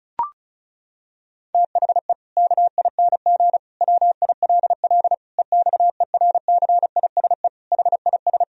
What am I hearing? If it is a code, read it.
Morse code, heard as a keyed tone, about 36 wpm, 700 Hz. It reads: THE KING WILL EXERCISE HIS
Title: Sentences from Top 1000 Words - Review Part 4 - 35wpm